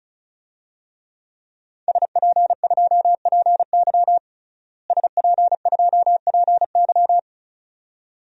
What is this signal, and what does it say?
Morse code, heard as a keyed tone, about 34 wpm, 700 Hz. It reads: SP2PY SP2PY
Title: DXCC-100 Call Signs - 5 Characters - Repeated 2x - 35wpm